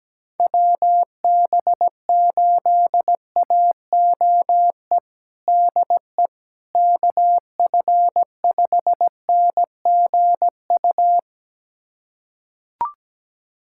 Morse code, heard as a keyed tone, about 17 wpm, 700 Hz. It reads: WB8AOE DE KF5NGU
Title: USA Callsign DE Callsign - 6 Characters - 17wpm